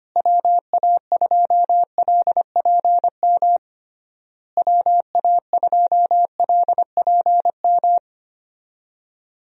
Morse code, heard as a keyed tone, about 25 wpm, 700 Hz. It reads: WA2LPM WA2LPM
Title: USA Call Signs - 6 Characters - Repeated 3x - 25wpm